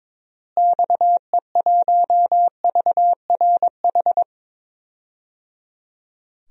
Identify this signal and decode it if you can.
Morse code, heard as a keyed tone, about 22 wpm, 700 Hz. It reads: XE1VRH
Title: DXCC-100 Call Signs - 6 Characters - Repeated 3x - 22wpm